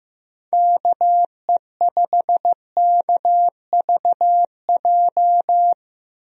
Morse code, heard as a keyed tone, about 15 wpm, 700 Hz. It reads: KE5KVJ
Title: USA Call Signs - 6 Characters - Repeated 3x - 15wpm